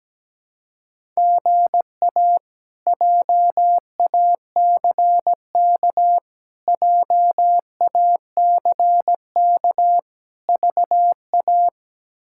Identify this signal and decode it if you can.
Morse code, heard as a keyed tone, about 17 wpm, 700 Hz. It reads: GA JACK JACK VA
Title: K1USN SST QSO Practice - 17wpm